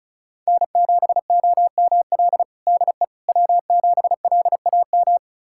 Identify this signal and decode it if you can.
Morse code, heard as a keyed tone, about 34 wpm, 700 Hz. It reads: N7OML DE W7LAM
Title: USA Callsign DE Callsign - 5 Characters - 35wpm